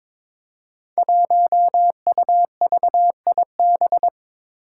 Morse code, heard as a keyed tone, about 21 wpm, 700 Hz. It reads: 1UVIB
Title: Five Challenging Letter-Number Combinations 22wpm